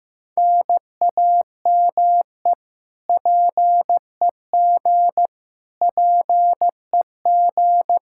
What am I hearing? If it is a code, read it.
Morse code, heard as a keyed tone, about 16 wpm, 700 Hz. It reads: NAME PEG PEG
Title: Practice QSO - Exchange 1 - Beginning - 15wpm